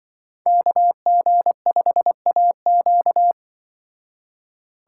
Morse code, heard as a keyed tone, about 25 wpm, 700 Hz. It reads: KG5AQ
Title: Speed Racing - USA 2x2 Call Signs 20wpm